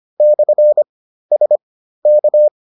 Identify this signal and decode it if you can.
Morse code, heard as a keyed tone, about 25 wpm, spaced as for 13 wpm, 600 Hz. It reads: /SK